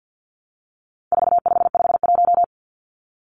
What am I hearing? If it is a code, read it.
Morse code, heard as a keyed tone, about 50 wpm, 700 Hz. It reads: 4551
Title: Numbers - 4-Digit 50wpm